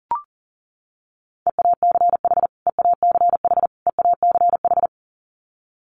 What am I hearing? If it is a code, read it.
Morse code, heard as a keyed tone, about 40 wpm, 700 Hz. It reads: EACH EACH EACH
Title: Top 300 Words Repeated 40wpm